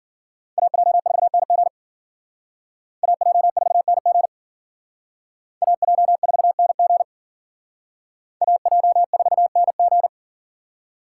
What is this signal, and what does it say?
Morse code, heard as a keyed tone, about 51 wpm, 700 Hz. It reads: AJ4NG AJ4NG AJ4NG AJ4NG